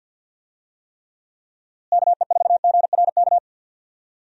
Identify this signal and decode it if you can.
Morse code, heard as a keyed tone, about 51 wpm, 700 Hz. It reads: KE4GRK
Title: USA Call Signs - 6 Characters - Repeated 2x - 50wpm